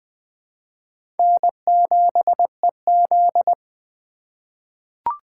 Morse code, heard as a keyed tone, about 20 wpm, 700 Hz. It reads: N7EZ
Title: USA Call Signs - 4 Characters - 4x Character Spacing - 20wpm